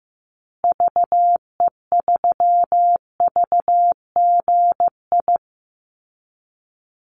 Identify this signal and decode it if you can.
Morse code, heard as a keyed tone, about 15 wpm, 700 Hz. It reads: VE3VGI
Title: Call Signs Repeated 2x - 15wpm